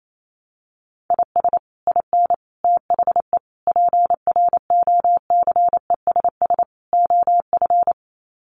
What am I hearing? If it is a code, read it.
Morse code, heard as a keyed tone, about 28 wpm, 700 Hz. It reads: IS IN THE PROCESS OF
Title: Sets of 5 Words Encore2 28wpm